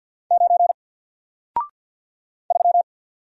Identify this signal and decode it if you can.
Morse code, heard as a keyed tone, about 48 wpm, 700 Hz. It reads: 9 3
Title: Single Character (Letter:Number:Punctuation:Prosign) Alt 50wpm